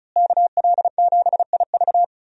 Morse code, heard as a keyed tone, about 36 wpm, 700 Hz. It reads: KL7IV
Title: USA Call Signs - 5 Characters - Repeated 3x - 35wpm